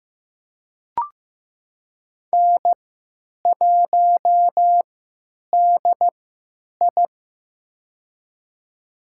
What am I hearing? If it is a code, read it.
Morse code, heard as a keyed tone, about 15 wpm, spaced as for 8 wpm, 700 Hz. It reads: N1DI
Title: USA Call Signs - 4 Characters - 3x Character Spacing - 15wpm